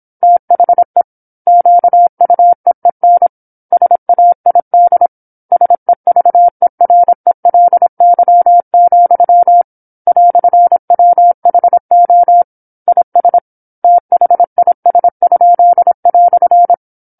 Morse code, heard as a keyed tone, about 27 wpm, 700 Hz. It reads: THE QUEEN SAID SEVERELY, "WHO IS THIS?"
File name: 08-Chapter-08 26 WPM